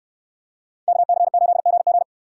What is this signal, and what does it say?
Morse code, heard as a keyed tone, about 69 wpm, 700 Hz. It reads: KC8GG